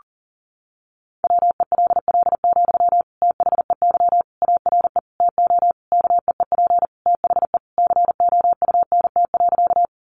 Morse code, heard as a keyed tone, about 40 wpm, 700 Hz. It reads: WELL, THEY ARE TO KEEP THE COUNT.
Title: Sentences from Top 1500 Words 40wpm